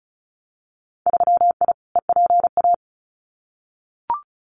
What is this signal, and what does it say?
Morse code, heard as a keyed tone, about 34 wpm, 700 Hz. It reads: 3I EPA
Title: Winter Field Day Exchanges 35wpm